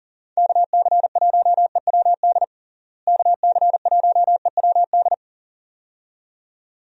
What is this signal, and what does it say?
Morse code, heard as a keyed tone, about 40 wpm, 700 Hz. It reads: KC1EWD KC1EWD
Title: USA Call Signs - 6 Characters - Repeated 2x - 40wpm